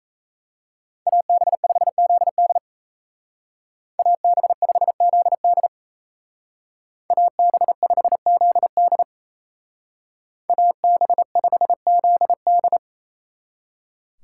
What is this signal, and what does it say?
Morse code, heard as a keyed tone, about 33 wpm, 700 Hz. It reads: AB5ZD AB5ZD AB5ZD AB5ZD
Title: Speed Racing - USA 2x2 Call Signs 28wpm